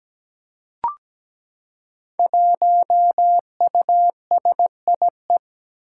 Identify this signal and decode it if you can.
Morse code, heard as a keyed tone, about 17 wpm, 700 Hz. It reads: 1USIE
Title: Five Challenging Letter-Number Combinations 17wpm